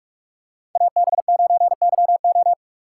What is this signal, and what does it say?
Morse code, heard as a keyed tone, about 49 wpm, 700 Hz. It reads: AB9YO